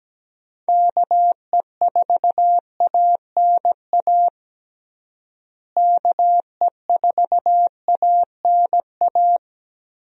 Morse code, heard as a keyed tone, about 17 wpm, 700 Hz. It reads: KE4ANA KE4ANA
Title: USA Call Signs - 6 Characters - Repeated 3x - 17wpm